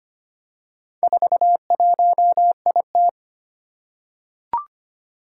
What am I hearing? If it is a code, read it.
Morse code, heard as a keyed tone, about 25 wpm, 700 Hz. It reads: 41IT
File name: practice / 25wpm / Four Letter-Number Combinations - 25wpm